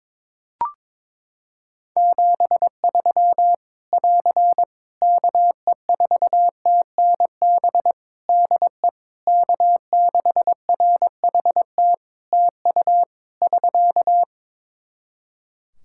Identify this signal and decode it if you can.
Morse code, heard as a keyed tone, about 22 wpm, 700 Hz. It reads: 73 <AR> KE4TNB DE K6RHT TU <SK>
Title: Practice QSO - Exchange 4 - Ending - 22wpm